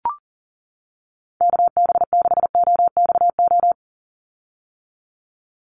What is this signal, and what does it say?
Morse code, heard as a keyed tone, about 40 wpm, 700 Hz. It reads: KB6OXO
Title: USA Call Signs 40wpm